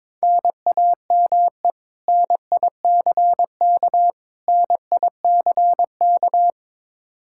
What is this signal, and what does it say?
Morse code, heard as a keyed tone, about 22 wpm, 700 Hz. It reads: NAME NICK NICK
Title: Practice QSO - Exchange 1 - Beginning - 22wpm